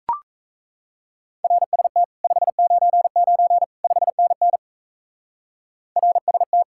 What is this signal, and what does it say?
Morse code, heard as a keyed tone, about 36 wpm, 700 Hz. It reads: RST 599 5NN RST